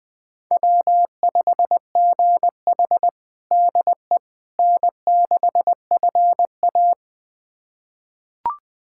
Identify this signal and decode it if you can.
Morse code, heard as a keyed tone, about 21 wpm, 700 Hz. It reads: W5GH DE N6FA
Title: USA Callsign DE Callsign - 4 Characters - 20wpm